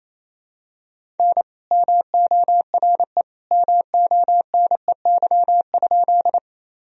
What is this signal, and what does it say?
Morse code, heard as a keyed tone, about 27 wpm, 700 Hz. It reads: N MORE MONEY?
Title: Sentences from Top 400 Words and Abbreviations 28wpm